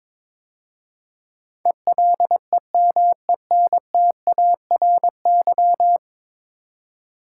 Morse code, heard as a keyed tone, about 21 wpm, 700 Hz. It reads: ELEMENTARY
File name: practice / 22wpm / 10 Letter Words - First 3 Letters Given - 22wpm